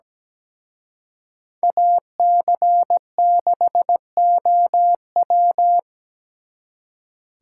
Morse code, heard as a keyed tone, about 20 wpm, 700 Hz. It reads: AC6OW
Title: Speed Racing - USA Call Signs 17wpm